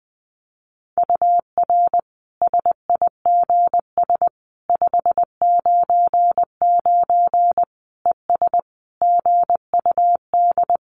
Morse code, heard as a keyed tone, about 20 wpm, 700 Hz. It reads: UR SIGS 599 ES GUD
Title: Common QSO Phrases 20wpm